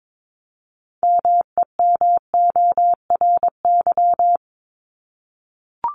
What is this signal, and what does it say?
Morse code, heard as a keyed tone, about 22 wpm, 700 Hz. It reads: MEMORY
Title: Top 1251-1500 Words 22wpm